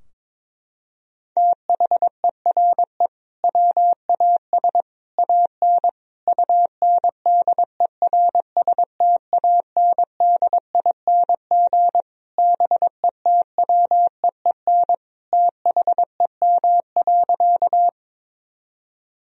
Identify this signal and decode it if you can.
Morse code, heard as a keyed tone, about 21 wpm, 700 Hz. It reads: THERE WAS AN UNDERSTANDING BETWEEN THEM.
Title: Sentences from Top 1000 Words - Review Part 3 - 22wpm